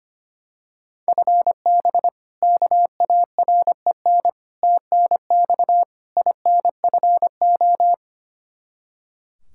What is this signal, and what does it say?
Morse code, heard as a keyed tone, about 25 wpm, 700 Hz. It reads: FB KAREN TNX INFO
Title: K1USN SST QSO Practice 25wpm